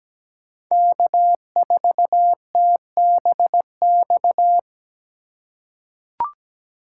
Morse code, heard as a keyed tone, about 17 wpm, 700 Hz. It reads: K4TBX
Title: USA Call Signs - 5 Characters - 2x Character Spacing - 17wpm